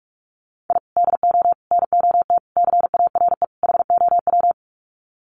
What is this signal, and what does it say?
Morse code, heard as a keyed tone, about 44 wpm, 700 Hz. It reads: I DO NOT CARE HOW
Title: Sets of 5 Words Encore2 45wpm